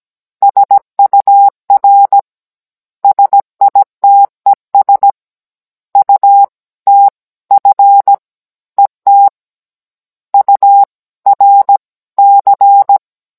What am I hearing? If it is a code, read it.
Morse code, heard as a keyed tone, about 17 wpm, 800 Hz. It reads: SUR SITES UTF ET URC
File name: Cours-NR19-16mm